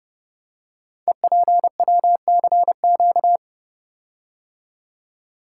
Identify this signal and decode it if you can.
Morse code, heard as a keyed tone, about 28 wpm, 700 Hz. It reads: EPWCQ
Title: Code Groups - 5 Letter:Numbers - 30wpm